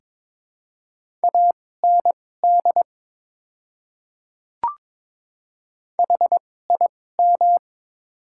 Morse code, heard as a keyed tone, about 22 wpm, spaced as for 16 wpm, 700 Hz. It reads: AND HIM
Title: 2x Character Spacing - Top 200 Words - 22wpm